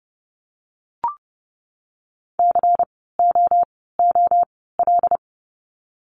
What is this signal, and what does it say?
Morse code, heard as a keyed tone, about 30 wpm, 700 Hz. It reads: C O O L
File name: Lesson 16f - Letter c - Words 30wpm